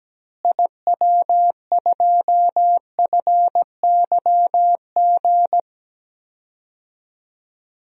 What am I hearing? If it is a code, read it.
Morse code, heard as a keyed tone, about 17 wpm, 700 Hz. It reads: IW2FYG
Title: DXCC-100 Call Signs - 6 Characters - Repeated 3x - 17wpm